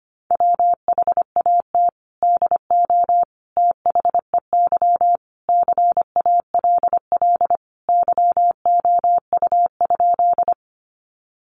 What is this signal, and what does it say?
Morse code, heard as a keyed tone, about 25 wpm, 700 Hz. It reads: WHAT DO THEY CALL YOU?
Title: Sentences from Top 100 Words 25wpm